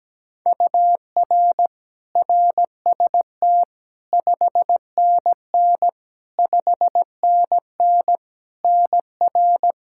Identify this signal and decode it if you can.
Morse code, heard as a keyed tone, about 17 wpm, 700 Hz. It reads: UR RST 5NN 5NN NR
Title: K1USN SST QSO Practice 17wpm